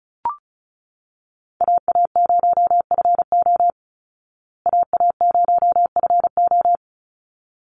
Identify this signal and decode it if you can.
Morse code, heard as a keyed tone, about 34 wpm, 700 Hz. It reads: AA0FO AA0FO
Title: Call Signs Repeated 3x - 35wpm